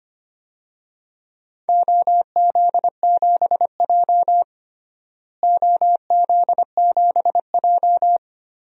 Morse code, heard as a keyed tone, about 25 wpm, 700 Hz. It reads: OZ7J OZ7J
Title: DXCC-100 Call Signs - 4 Characters - Repeated 2x - 25wpm